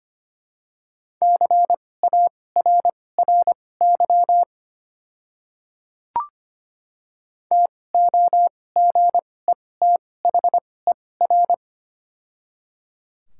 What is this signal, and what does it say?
Morse code, heard as a keyed tone, about 25 wpm, spaced as for 18 wpm, 700 Hz. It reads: CARRY TOGETHER
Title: 2x Character Spacing - Top 300 Words - 25wpm